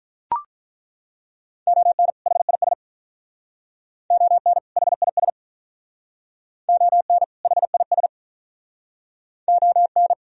Morse code, heard as a keyed tone, about 43 wpm, 700 Hz. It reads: ON HIS ON HIS ON HIS ON
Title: Speed-Racing - Sets of 2 Words 35wpm